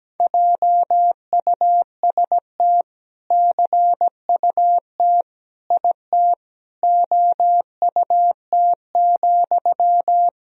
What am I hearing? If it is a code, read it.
Morse code, heard as a keyed tone, about 17 wpm, 700 Hz. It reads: JUST CUT IT OUT,
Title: Sentences from Top 300 Words and Abbreviations 17wpm